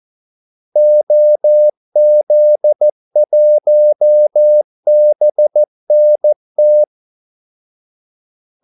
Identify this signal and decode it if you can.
Morse code, heard as a keyed tone, about 14 wpm, 600 Hz. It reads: OZ1BNT
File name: Callsigns-14wpm